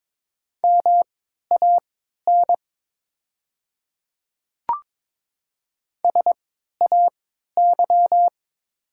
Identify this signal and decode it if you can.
Morse code, heard as a keyed tone, about 22 wpm, spaced as for 12 wpm, 700 Hz. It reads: MAN SAY
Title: 3x Character Spacing - Top 200 Words - 22wpm